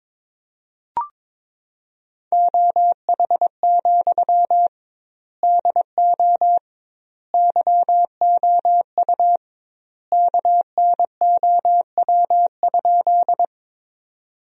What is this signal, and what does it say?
Morse code, heard as a keyed tone, about 22 wpm, 700 Hz. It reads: OH, DO YOU KNOW?
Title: 2x Word-Spacing - Sentences from Top 500 Words - 22wpm